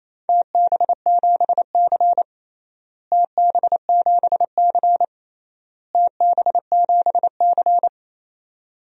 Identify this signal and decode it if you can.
Morse code, heard as a keyed tone, about 28 wpm, 700 Hz. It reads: TB7C TB7C TB7C
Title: DXCC-100 Call Signs - 4 Characters - Repeated 3x - 28wpm